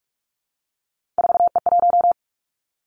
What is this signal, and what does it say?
Morse code, heard as a keyed tone, about 46 wpm, 700 Hz. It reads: 4E1
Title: Three Challenging Letter-Number Combinations 45wpm